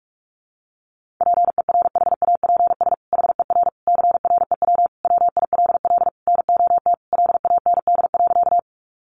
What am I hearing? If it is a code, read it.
Morse code, heard as a keyed tone, about 44 wpm, 700 Hz. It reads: PERHAPS HER CREW WILL NOT LAND.
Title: Sentences from Top 1500 Words 45wpm